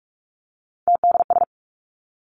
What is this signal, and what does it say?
Morse code, heard as a keyed tone, about 46 wpm, 700 Hz. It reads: TDS
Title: Three Challenging Letter-Number Combinations 45wpm